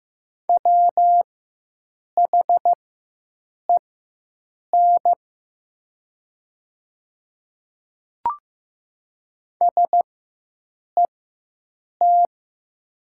Morse code, heard as a keyed tone, about 15 wpm, spaced as for 7 wpm, 700 Hz. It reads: WHEN SET
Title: 4x Character Spacing - Top 500 Words - 15wpm